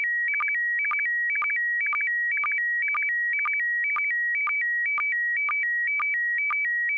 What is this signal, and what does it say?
SSTV (slow-scan television), PD 120.
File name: music / example / sstv pd120